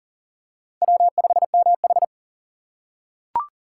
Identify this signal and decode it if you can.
Morse code, heard as a keyed tone, about 40 wpm, 700 Hz. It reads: W5MH
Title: USA Call Signs - 4 Characters - 4x Character Spacing - 40wpm